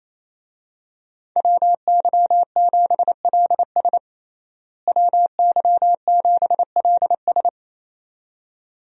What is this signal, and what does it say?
Morse code, heard as a keyed tone, about 28 wpm, 700 Hz. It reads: WY7LS WY7LS
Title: USA Call Signs - 5 Characters - Repeated 2x - 28wpm